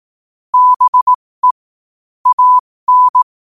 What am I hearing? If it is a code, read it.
Morse code, heard as a keyed tone, about 18 wpm, spaced as for 15 wpm, 1000 Hz. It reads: BE AN
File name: qotdfiles / 15wpm